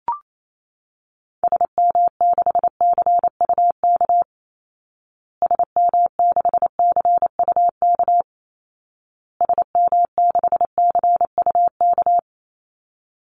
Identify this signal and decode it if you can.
Morse code, heard as a keyed tone, about 28 wpm, 700 Hz. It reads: SM6CUK SM6CUK SM6CUK
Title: Call Signs Repeated 2x - 28wpm